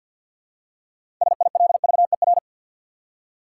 Morse code, heard as a keyed tone, about 51 wpm, 700 Hz. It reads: SILVER
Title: Six Letter Words - 50wpm